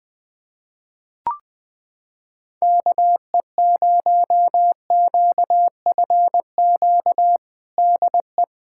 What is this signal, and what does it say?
Morse code, heard as a keyed tone, about 20 wpm, 700 Hz. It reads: KE0QFQ DE
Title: USA Callsign DE Callsign 20wpm